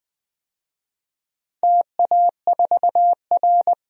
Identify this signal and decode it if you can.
Morse code, heard as a keyed tone, about 20 wpm, 700 Hz. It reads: TA4R
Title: DXCC-100 Call Signs - 4 Characters - 20wpm